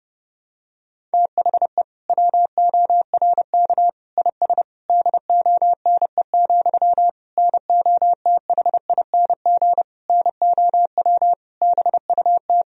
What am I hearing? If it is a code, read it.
Morse code, heard as a keyed tone, about 30 wpm, 700 Hz. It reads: THE WORK IS DONE, NOTHING NOW BUT
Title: Sentences from Top 1000 Words - Review Part 4 - 30wpm